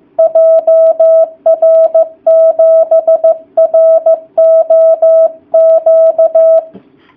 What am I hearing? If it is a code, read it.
Morse code, heard as a keyed tone, about 15 wpm, 660 Hz. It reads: JR7ROQ
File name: jh8cbh / 2019cw / sp15- w